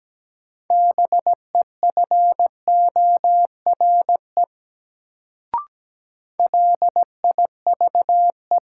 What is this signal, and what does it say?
Morse code, heard as a keyed tone, about 17 wpm, 700 Hz. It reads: BEFORE LIVE
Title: Sentences from Top 200 Words - New Word Drill - 17wpm